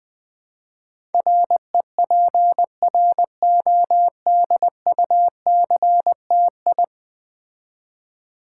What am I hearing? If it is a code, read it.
Morse code, heard as a keyed tone, about 21 wpm, 700 Hz. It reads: REPRODUCTI
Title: Build A Word - 5 Letters At A Time - 12-Letter Words - 20wpm